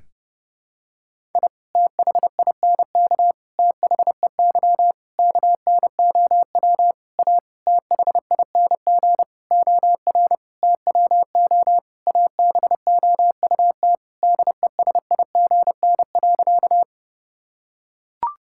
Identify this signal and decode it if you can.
Morse code, heard as a keyed tone, about 30 wpm, 700 Hz. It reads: I THINK THEY KNOW A THING OR TWO ABOUT DESIGN.
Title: Sentences from Top 1000 Words - Review Part 5 - 30wpm